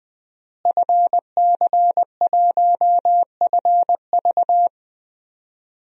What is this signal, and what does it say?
Morse code, heard as a keyed tone, about 20 wpm, 700 Hz. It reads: FC1FV
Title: DXCC-100 Call Signs - 5 Characters - Repeated 3x - 20wpm